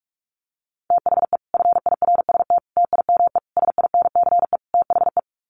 Morse code, heard as a keyed tone, about 44 wpm, 700 Hz. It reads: THE FIRST TIME SINCE THE
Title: Sets of 5 Words Encore 45wpm